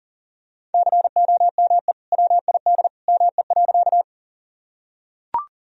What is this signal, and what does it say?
Morse code, heard as a keyed tone, about 40 wpm, 700 Hz. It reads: COME WID ME.
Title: Sentences from Top 200 Words and Abbreviations 40wpm